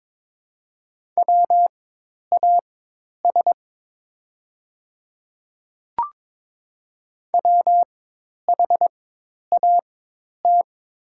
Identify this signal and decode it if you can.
Morse code, heard as a keyed tone, about 22 wpm, spaced as for 10 wpm, 690 Hz. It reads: WAS WHAT
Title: 4x Character Spacing - Top 100 Words - 22wpm